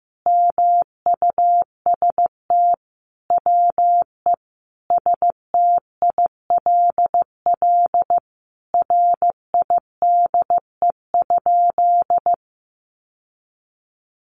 Morse code, heard as a keyed tone, about 15 wpm, 700 Hz. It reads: MUST WE STILL RIDE?
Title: Sentences from Top 1200 Words 15wpm